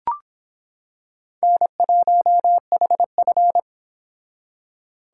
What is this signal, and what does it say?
Morse code, heard as a keyed tone, about 26 wpm, 700 Hz. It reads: N1HF